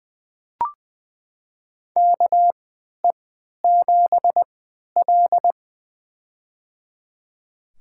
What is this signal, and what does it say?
Morse code, heard as a keyed tone, about 20 wpm, spaced as for 11 wpm, 700 Hz. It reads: KE7L
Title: USA Call Signs - 4 Characters - 3x Character Spacing - 20wpm